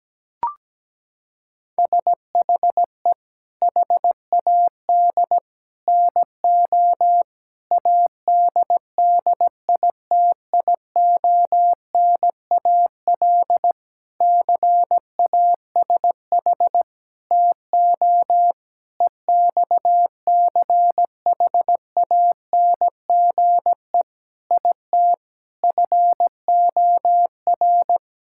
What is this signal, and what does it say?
Morse code, heard as a keyed tone, about 17 wpm, 700 Hz. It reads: SHE HAD NO ADDITIONAL CASH TO EXCHANGE IT FOR
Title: Sentences from Top 1000 Words - Review Part 6 - 17wpm